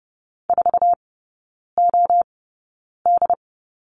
Morse code, heard as a keyed tone, about 30 wpm, spaced as for 9 wpm, 700 Hz. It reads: 4OD
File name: course / 30:10wpm / 087 Lesson 15e - Number 5 - Three Characters 30wpm